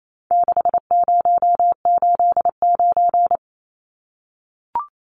Morse code, heard as a keyed tone, about 28 wpm, 700 Hz. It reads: 6089
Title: Numbers - 4-Digit 28wpm